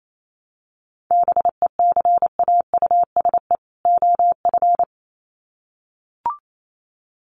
Morse code, heard as a keyed tone, about 28 wpm, 700 Hz. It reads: BECAUSE OF
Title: Sets of 2 Words 28wpm